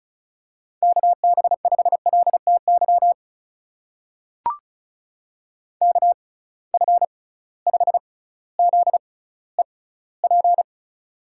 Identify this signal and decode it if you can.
Morse code, heard as a keyed tone, about 36 wpm, spaced as for 18 wpm, 700 Hz. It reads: KB5LTY KF5ZEP
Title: USA Call Signs - 6 Characters - 6x Character Spacing - 35wpm